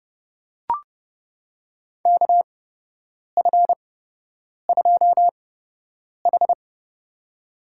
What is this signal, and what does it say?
Morse code, heard as a keyed tone, about 30 wpm, spaced as for 8 wpm, 700 Hz. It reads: KF2H
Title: USA Call Signs - 4 Characters - 8x Character Spacing - 30wpm